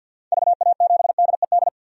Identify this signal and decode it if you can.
Morse code, heard as a keyed tone, about 48 wpm, 700 Hz. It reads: VA7DED
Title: DXCC-100 Call Signs - 6 Characters - 50wpm